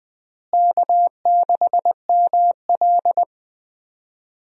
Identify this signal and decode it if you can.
Morse code, heard as a keyed tone, about 20 wpm, 700 Hz. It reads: K6ML
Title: USA Call Signs - 4 Characters - Repeated 2x - 20wpm